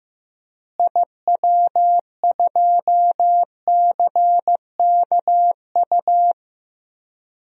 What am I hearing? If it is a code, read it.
Morse code, heard as a keyed tone, about 15 wpm, 700 Hz. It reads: IW2CKU
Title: DXCC-100 Call Signs - 6 Characters - Repeated 2x - 15wpm